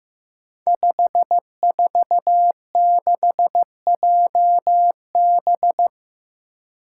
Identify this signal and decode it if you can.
Morse code, heard as a keyed tone, about 15 wpm, 700 Hz. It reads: 546JB
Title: Five Challenging Letter-Number Combinations 15wpm